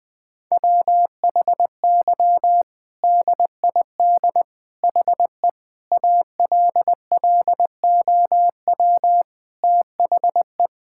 Morse code, heard as a keyed tone, about 21 wpm, 700 Hz. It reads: WHY DID HE ALLOW THE
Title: Sentences from Top 1000 Words - Review Part 5 - 20wpm